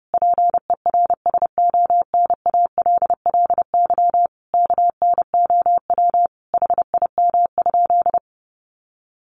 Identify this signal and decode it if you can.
Morse code, heard as a keyed tone, about 30 wpm, 700 Hz. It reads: PERSONALLY KNOW HIM?
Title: Sentences from Top 1400 Words 30wpm